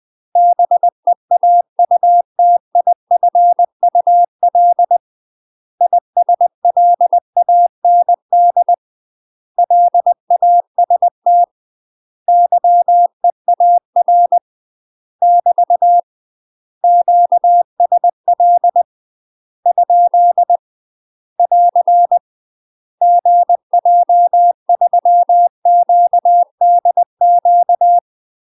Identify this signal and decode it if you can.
Morse code, heard as a keyed tone, about 20 wpm, 700 Hz. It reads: BEAUTIFUL ISLAND LAST YEAR <BT> QSL ? <AR> GJ3QDQ
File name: UK MASTER QSOs TRAINING / 20WPM / UK MASTER QSOs 0048